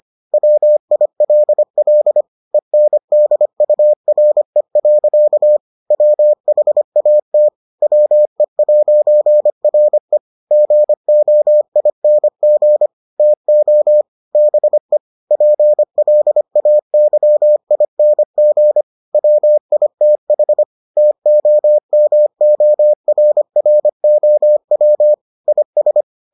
Morse code, heard as a keyed tone, about 25 wpm, 600 Hz. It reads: WILL ENDURE. WHAT WE'RE GOING TO BE PLAYING WITH TOMORROW IS